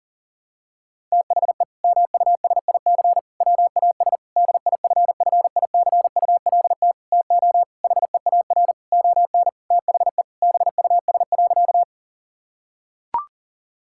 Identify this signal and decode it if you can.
Morse code, heard as a keyed tone, about 40 wpm, 700 Hz. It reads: THE MUSIC WAS DIFFICULT TO HEAR ON THE BUS.
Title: Sentences from Top 1000 Words - Review Part 6 - 40wpm